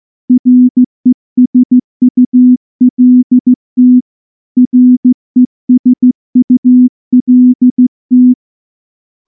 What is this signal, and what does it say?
Morse code, heard as a keyed tone, about 14 wpm, 250 Hz. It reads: RESULT RESULT
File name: morse-training-15WPM